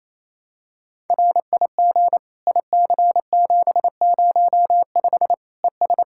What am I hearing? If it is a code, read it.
Morse code, heard as a keyed tone, about 28 wpm, 700 Hz. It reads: RIG IC705 ES
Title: K1USN SST QSO Practice 28wpm